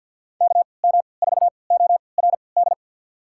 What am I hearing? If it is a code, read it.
Morse code, heard as a keyed tone, about 52 wpm, spaced as for 29 wpm, 700 Hz. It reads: KM4ORD
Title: USA Call Signs - 6 Characters - 3x Character Spacing - 50wpm